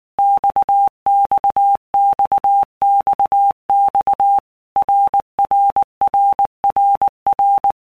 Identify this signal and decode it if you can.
Morse code, heard as a keyed tone, about 19 wpm, 800 Hz. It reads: XXXXX RRRRR
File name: All Groups 1-minute Blocks / Groups@18 / RKPX